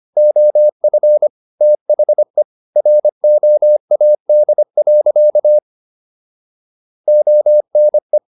Morse code, heard as a keyed tone, about 25 wpm, 600 Hz. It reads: OF THE ROAD. ONE